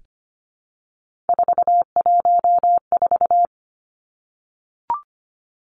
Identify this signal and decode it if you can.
Morse code, heard as a keyed tone, about 25 wpm, 700 Hz. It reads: 414
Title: Numbers - 3-Digit 25wpm